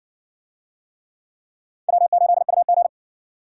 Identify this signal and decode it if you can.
Morse code, heard as a keyed tone, about 60 wpm, 700 Hz. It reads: W8UG